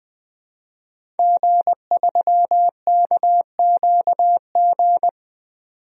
Morse code, heard as a keyed tone, about 20 wpm, 700 Hz. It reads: G3KQG
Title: DXCC-100 Call Signs - 5 Characters - 20wpm